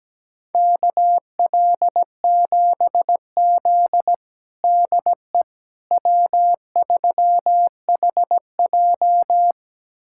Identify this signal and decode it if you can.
Morse code, heard as a keyed tone, about 17 wpm, 700 Hz. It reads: KL7Z DE W3HJ
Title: USA Callsign DE Callsign - 4 Characters - 17wpm